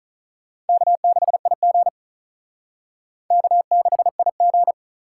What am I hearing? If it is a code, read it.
Morse code, heard as a keyed tone, about 42 wpm, 700 Hz. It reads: K6IG K6IG
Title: Speed Racing - USA 1x2 Call Signs 35wpm